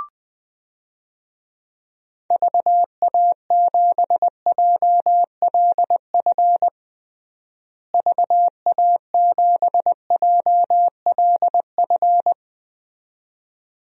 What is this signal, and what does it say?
Morse code, heard as a keyed tone, about 21 wpm, 700 Hz. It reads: VA7JLF VA7JLF
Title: DXCC-100 Call Signs - 6 Characters - Repeated 2x - 20wpm